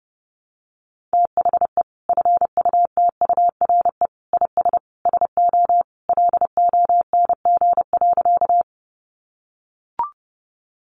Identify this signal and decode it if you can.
Morse code, heard as a keyed tone, about 30 wpm, 700 Hz. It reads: THE FUTURE IS SO LONG.
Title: Sentences from Top 700 Words 30wpm